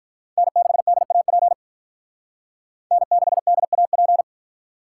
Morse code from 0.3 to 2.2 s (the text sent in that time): N6DAP